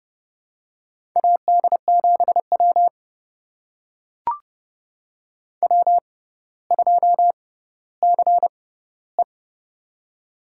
Morse code, heard as a keyed tone, about 30 wpm, spaced as for 15 wpm, 700 Hz. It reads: AD7W W2CE
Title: USA Call Signs - 4 Characters - 6x Character Spacing - 30wpm